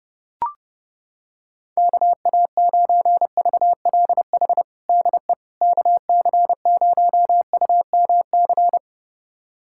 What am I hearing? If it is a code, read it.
Morse code, heard as a keyed tone, about 30 wpm, 700 Hz. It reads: KA9VLH DE KC0UMC
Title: USA Callsign DE Callsign - 6 Characters - 30wpm